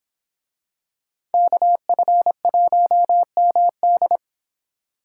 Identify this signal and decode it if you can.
Morse code, heard as a keyed tone, about 25 wpm, 700 Hz. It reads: KF1MD